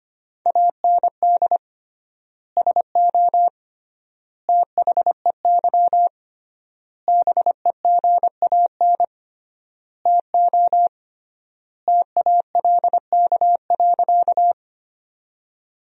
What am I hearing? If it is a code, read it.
Morse code, heard as a keyed tone, about 25 wpm, 700 Hz. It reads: AND SO THEY BEGAN TO TALK.
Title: 3x Word-Spacing - Sentences from Top 400 Words - 25wpm